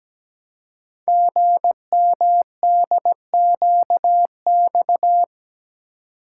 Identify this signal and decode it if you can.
Morse code, heard as a keyed tone, about 17 wpm, 700 Hz. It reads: GMDQX
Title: Code Groups - 5 Letters - 17wpm